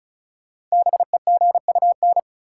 Morse code, heard as a keyed tone, about 36 wpm, 700 Hz. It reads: BEGUN
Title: Sentences from Top 1000 Words - Review Part 6 - New Word Drill - 35wpm